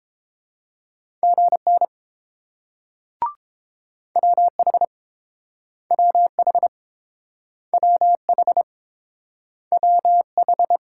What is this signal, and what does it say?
Morse code, heard as a keyed tone, about 30 wpm, 690 Hz. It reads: GN WH WH WH WH